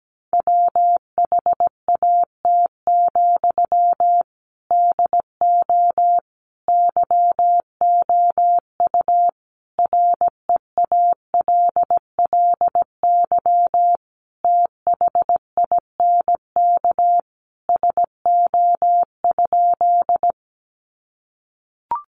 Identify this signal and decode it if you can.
Morse code, heard as a keyed tone, about 17 wpm, 700 Hz. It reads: WHAT, DO YOU REALLY THINK SO?
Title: Sentences from Top 600 Words 17wpm